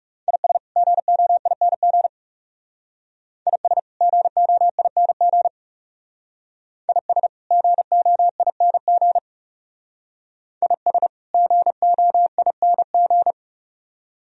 Morse code, heard as a keyed tone, about 37 wpm, 700 Hz. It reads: IS GOING IS GOING IS GOING IS GOING
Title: Speed-Racing - Sets of 2 Words 30wpm